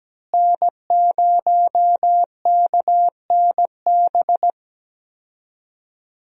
Morse code, heard as a keyed tone, about 17 wpm, 700 Hz. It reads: N0KNB
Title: USA Call Signs - 5 Characters - Repeated 2x - 17wpm